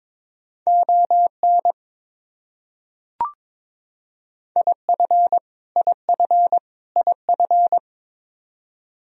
Morse code, heard as a keyed tone, about 22 wpm, 700 Hz. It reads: ON IF IF IF
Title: Top 100 Words Repeated 22wpm